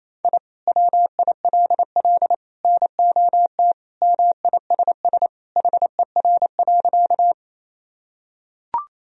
Morse code, heard as a keyed tone, about 28 wpm, 700 Hz. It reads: I WILL NOT MISS HER.
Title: Sentences from Top 1000 Words - Review Part 3 - 28wpm